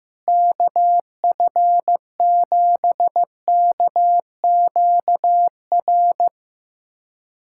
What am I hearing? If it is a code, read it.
Morse code, heard as a keyed tone, about 15 wpm, 700 Hz. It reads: KF7KQR
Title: USA Callsign DE Callsign 15wpm